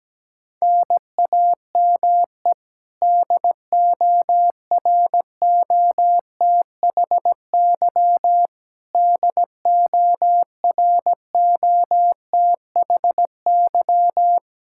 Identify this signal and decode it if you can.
Morse code, heard as a keyed tone, about 17 wpm, 700 Hz. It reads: NAME DOROTHY DOROTHY
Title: Practice QSO - Exchange 1 - Beginning - 17wpm